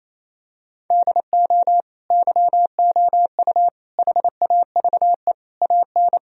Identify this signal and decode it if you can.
Morse code, heard as a keyed tone, about 28 wpm, 700 Hz. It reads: DO YOU HAVE AN
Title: Sentences from Top 1000 Words - Review Part 3 - 28wpm